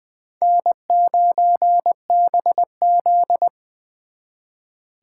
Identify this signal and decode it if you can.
Morse code, heard as a keyed tone, about 24 wpm, 700 Hz. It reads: N9BZ
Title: Speed Racing - USA 1x2 Call Signs 20wpm